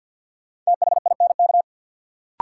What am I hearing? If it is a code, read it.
Morse code, heard as a keyed tone, about 53 wpm, 700 Hz. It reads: THINK
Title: Sets of 2 Words - New Word Drill - 50wpm